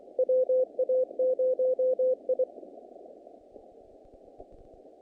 Morse code, heard as a keyed tone, about 24 wpm, 510 Hz. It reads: WA0I
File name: contests / naqp-cw-k5tr-op-k5pi- -2100